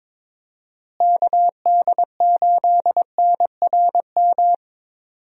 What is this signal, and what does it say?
Morse code, heard as a keyed tone, about 22 wpm, 700 Hz. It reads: KD8NRM
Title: USA Call Signs - 6 Characters - Repeated 2x - 22wpm